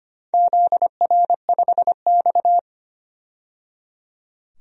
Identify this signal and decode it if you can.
Morse code, heard as a keyed tone, about 25 wpm, 700 Hz. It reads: ZR5X
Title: DXCC-100 Call Signs - 4 Characters - 25wpm